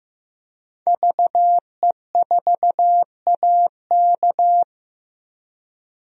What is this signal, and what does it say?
Morse code, heard as a keyed tone, about 15 wpm, 700 Hz. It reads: VE4AK
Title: DXCC-100 Call Signs - 5 Characters - 15wpm